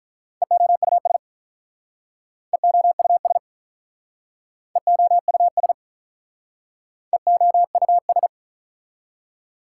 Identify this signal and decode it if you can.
Morse code, heard as a keyed tone, about 45 wpm, 690 Hz. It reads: EOUS EOUS EOUS EOUS